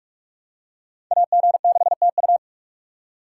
Morse code, heard as a keyed tone, about 39 wpm, 700 Hz. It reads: AG6TU